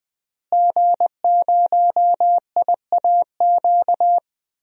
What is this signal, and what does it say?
Morse code, heard as a keyed tone, about 20 wpm, 700 Hz. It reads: G0IAQ
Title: DXCC-100 Call Signs - 5 Characters - Repeated 3x - 20wpm